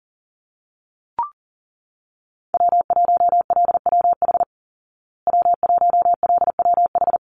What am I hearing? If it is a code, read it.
Morse code, heard as a keyed tone, about 40 wpm, 700 Hz. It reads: W1LWH W1LWH
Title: Call Signs Repeated 2x - 40wpm